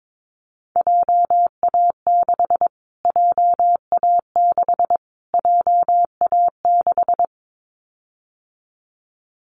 Morse code, heard as a keyed tone, about 22 wpm, 700 Hz. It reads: JA6 JA6 JA6
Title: Callsign Prefixes for Top 25 DXCC - Repeated 3x - 22wpm